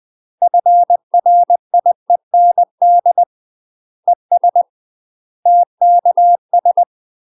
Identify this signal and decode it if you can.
Morse code, heard as a keyed tone, about 20 wpm, 700 Hz. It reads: FRIEND ES TKS
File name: UK MASTER QSOs TRAINING/20WPM